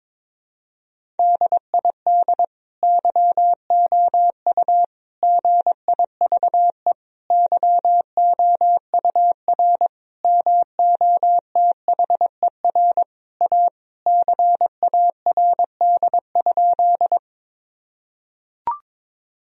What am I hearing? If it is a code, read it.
Morse code, heard as a keyed tone, about 22 wpm, 700 Hz. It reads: DID YOU GIVE YOUR MOTHER A CARD?
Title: Sentences from Top 1000 Words - Review Part 4 - 22wpm